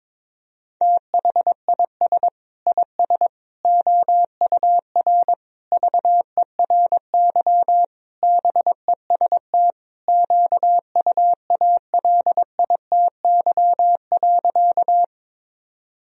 Morse code, heard as a keyed tone, about 22 wpm, 700 Hz. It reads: THIS IS OUR VERY BEST QUALITY.
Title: Sentences from Top 1000 Words - Review Part 4 - 22wpm